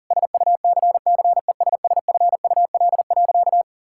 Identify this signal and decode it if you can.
Morse code, heard as a keyed tone, about 40 wpm, 700 Hz. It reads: SUCCESSFUL.
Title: Sentences from Top 1000 Words - Review Part 6 - 40wpm